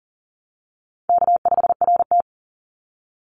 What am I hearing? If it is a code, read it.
Morse code, heard as a keyed tone, about 40 wpm, 700 Hz. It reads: K5RT
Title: Call Signs - 40wpm